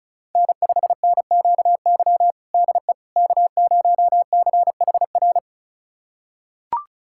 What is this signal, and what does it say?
Morse code, heard as a keyed tone, about 34 wpm, 700 Hz. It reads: N5NQY DE K0CHR
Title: USA Callsign DE Callsign - 5 Characters - 35wpm